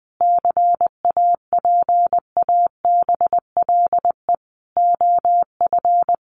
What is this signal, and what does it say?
Morse code, heard as a keyed tone, about 20 wpm, 700 Hz. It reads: CAPABLE OF
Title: Sentences from Top 1200 Words 20wpm